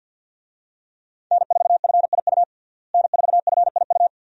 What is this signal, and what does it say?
Morse code, heard as a keyed tone, about 52 wpm, 700 Hz. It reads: N4FIU N4FIU
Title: USA Call Signs - 5 Characters - Repeated 2x - 50wpm